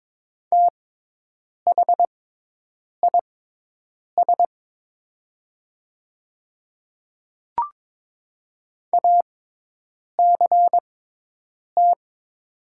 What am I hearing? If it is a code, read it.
Morse code, heard as a keyed tone, about 22 wpm, spaced as for 8 wpm, 700 Hz. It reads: THIS ACT
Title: 6x Character Spacing - Top 200 Words - 22wpm